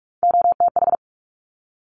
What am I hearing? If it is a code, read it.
Morse code, heard as a keyed tone, about 46 wpm, 700 Hz. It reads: OTH
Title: Top Three Letter Combinations in Words 45wpm